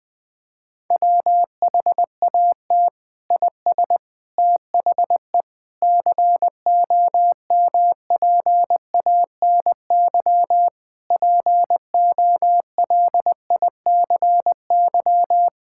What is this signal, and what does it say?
Morse code, heard as a keyed tone, about 20 wpm, 700 Hz. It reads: WHAT IS THE COMPANY POLICY
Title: Sentences from Top 1000 Words - Review Part 5 - 20wpm